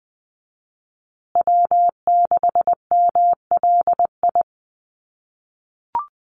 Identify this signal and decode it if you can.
Morse code, heard as a keyed tone, about 20 wpm, 700 Hz. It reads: W6MLI
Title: USA Call Signs Repeated 2x 20wpm